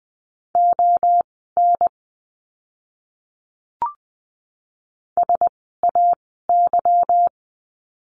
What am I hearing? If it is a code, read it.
Morse code, heard as a keyed tone, about 20 wpm, spaced as for 14 wpm, 700 Hz. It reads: ON SAY
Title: 185 Lesson 30f - Letter b - Words 20wpm